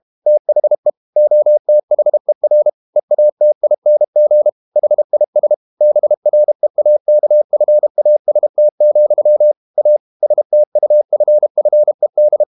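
Morse code, heard as a keyed tone, about 32 wpm, 600 Hz. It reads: THE OTHER EATING HIS BREAKFAST, A STUFFED
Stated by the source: Morse code practice at 32 wpm from CW med Gnister, reading Ken Follett's novel Never in English.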